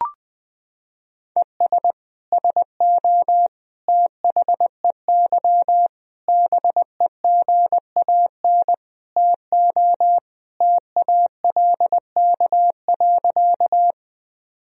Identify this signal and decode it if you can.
Morse code, heard as a keyed tone, about 21 wpm, 700 Hz. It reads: ES SO THEY BEGAN TO TALK.
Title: Sentences from Top 400 Words and Abbreviations 20wpm